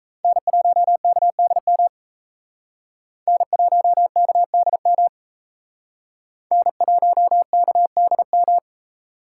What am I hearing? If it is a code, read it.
Morse code, heard as a keyed tone, about 35 wpm, 700 Hz. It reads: N1KDM N1KDM N1KDM
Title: Speed Racing - USA 1x3 Call Signs 28wpm